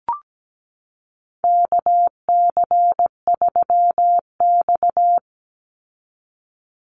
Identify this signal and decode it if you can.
Morse code, heard as a keyed tone, about 17 wpm, 700 Hz. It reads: KC3X
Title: Call Signs Repeated 3x - 17wpm